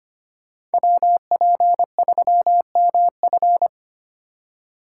Morse code, heard as a keyed tone, about 33 wpm, 700 Hz. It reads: WP3MF